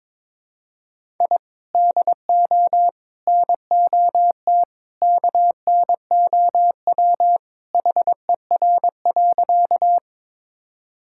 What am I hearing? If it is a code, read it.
Morse code, heard as a keyed tone, about 21 wpm, 700 Hz. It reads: I DO NOT KNOW HER.
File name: practice / 22wpm / Sentences with 2-Word Phrases - 22wpm